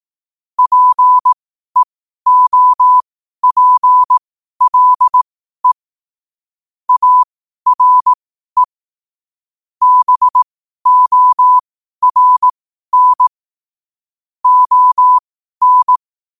Morse code, heard as a keyed tone, about 18 wpm, spaced as for 12 wpm, 1000 Hz. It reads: PEOPLE ARE BORN ON
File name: qotdfiles / 12wpm